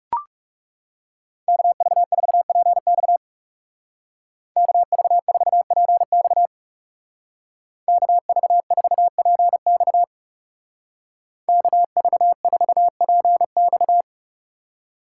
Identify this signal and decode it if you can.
Morse code, heard as a keyed tone, about 36 wpm, 700 Hz. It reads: KV4PX KV4PX KV4PX KV4PX
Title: Speed Racing - USA 2x2 Call Signs 30wpm